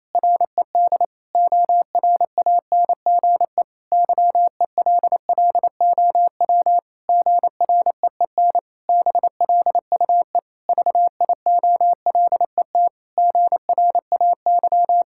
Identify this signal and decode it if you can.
Morse code, heard as a keyed tone, about 28 wpm, 700 Hz. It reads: RED ORANGE YELLOW GREEN BLUE VIOLET GRAY
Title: Pangram Challenge v2 - 28wpm